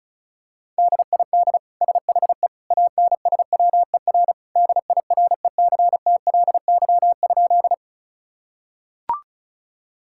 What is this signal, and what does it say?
Morse code, heard as a keyed tone, about 36 wpm, 700 Hz. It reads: DID SHE ANSWER DIRECTLY?
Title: Sentences from Top 1000 Words - Review Part 5 - 35wpm